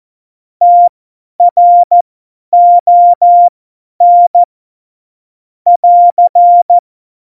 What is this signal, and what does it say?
Morse code, heard as a keyed tone, about 14 wpm, spaced as for 10 wpm, 700 Hz. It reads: TRON <AR>
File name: Morsekurs / Witze / Witz 23. Oktober 2018, 50 F70